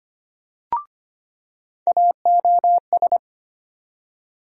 Morse code, heard as a keyed tone, about 25 wpm, 700 Hz. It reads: AOS
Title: USA Call Sign Postfixes - 3 Letters 25wpm